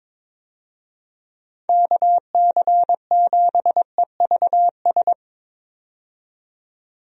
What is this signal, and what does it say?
Morse code, heard as a keyed tone, about 22 wpm, 700 Hz. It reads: KC7EVS
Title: USA Call Signs - 6 Characters - 22wpm